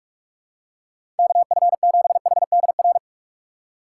Morse code, heard as a keyed tone, about 46 wpm, 700 Hz. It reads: KF7HDR
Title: USA Call Signs - 6 Characters - 2x Character Spacing - 45wpm